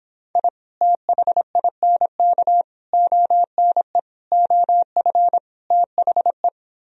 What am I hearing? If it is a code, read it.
Morse code, heard as a keyed tone, about 26 wpm, 700 Hz. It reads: I THINK ONE OF THE